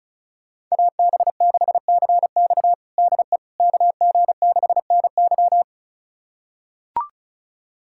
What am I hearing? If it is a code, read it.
Morse code, heard as a keyed tone, about 34 wpm, 690 Hz. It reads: AB6CX DE KG6NY
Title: USA Callsign DE Callsign - 5 Characters - 35wpm